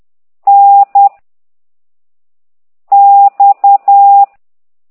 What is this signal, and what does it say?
Morse code, heard as a keyed tone, about 10 wpm, 800 Hz. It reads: N X
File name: ct2idw / exerc11